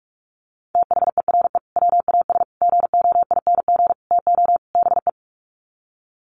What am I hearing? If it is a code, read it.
Morse code, heard as a keyed tone, about 44 wpm, 700 Hz. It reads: THERE WAS GOING TO BE
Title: Sets of 5 Words Encore2 45wpm